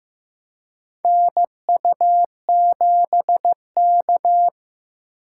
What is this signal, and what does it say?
Morse code, heard as a keyed tone, about 15 wpm, 700 Hz. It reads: NU7K
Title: USA Call Signs - 4 Characters - 3x Character Spacing - 15wpm